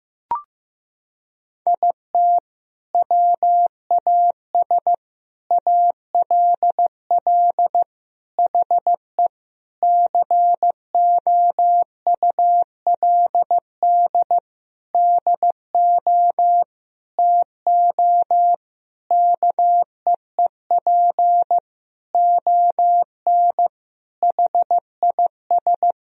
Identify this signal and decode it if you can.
Morse code, heard as a keyed tone, about 15 wpm, 700 Hz. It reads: IT WAS ALL HE COULD DO TO KEEP ON HIS
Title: Sentences from Top 1000 Words - Review Part 1 - 15wpm